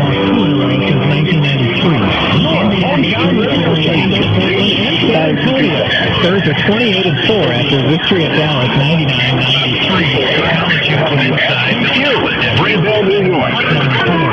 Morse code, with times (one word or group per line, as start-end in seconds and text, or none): none